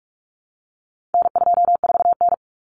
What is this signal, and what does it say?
Morse code, heard as a keyed tone, about 44 wpm, 700 Hz. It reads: N24N
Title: Common Cut Numbers - 4-Digit 45wpm